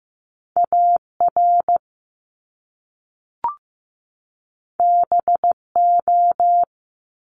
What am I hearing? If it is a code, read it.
Morse code, heard as a keyed tone, about 15 wpm, 700 Hz. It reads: AR BO
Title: Top Two Letter Combinations in Words 15wpm